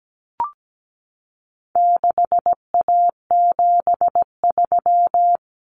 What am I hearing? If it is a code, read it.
Morse code, heard as a keyed tone, about 17 wpm, 700 Hz. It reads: 6A73
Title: Common Cut Numbers - 4-Digit 17wpm